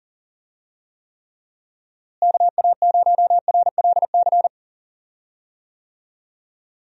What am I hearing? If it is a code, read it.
Morse code, heard as a keyed tone, about 37 wpm, 700 Hz. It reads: KA0RLC
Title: USA Call Signs - 6 Characters - Repeated 3x - 40wpm